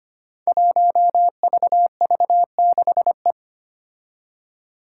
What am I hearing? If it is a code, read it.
Morse code, heard as a keyed tone, about 25 wpm, 700 Hz. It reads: 1VV6E
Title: Five Challenging Letter-Number Combinations 25wpm